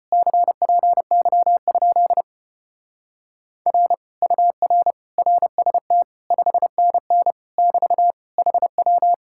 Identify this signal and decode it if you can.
Morse code, heard as a keyed tone, about 29 wpm, 700 Hz. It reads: CPY? R UR RST 5NN <BT> HW
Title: Speed-Racing Common QSO Phrases Encore - 25wpm